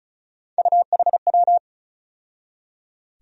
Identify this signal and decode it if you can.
Morse code, heard as a keyed tone, about 34 wpm, 700 Hz. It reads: UHW
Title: USA Call Sign Postfixes - 3 Letters 35wpm